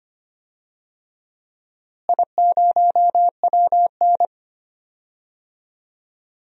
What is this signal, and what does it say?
Morse code, heard as a keyed tone, about 25 wpm, 700 Hz. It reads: I0WN
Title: DXCC-100 Call Signs - 4 Characters - Repeated 3x - 25wpm